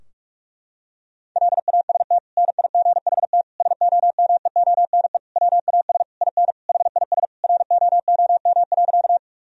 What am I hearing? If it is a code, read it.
Morse code, heard as a keyed tone, about 44 wpm, 700 Hz. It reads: LAST NIGHT SOMEONE WAS IN HIS ROOM.
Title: Sentences from Top 1000 Words - Review Part 4 - 45wpm